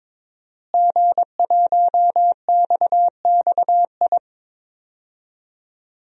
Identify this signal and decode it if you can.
Morse code, heard as a keyed tone, about 22 wpm, 700 Hz. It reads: G1XXI
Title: DXCC-100 Call Signs - 5 Characters - Repeated 2x - 22wpm